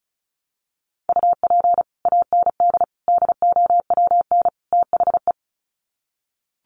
Morse code, heard as a keyed tone, about 34 wpm, 700 Hz. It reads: UP AND DOWN THE
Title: Sets of 4 Words 35wpm